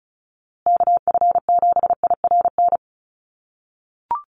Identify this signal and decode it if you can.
Morse code, heard as a keyed tone, about 34 wpm, 700 Hz. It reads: KF7IRN
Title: USA Call Signs Repeated 2x 35wpm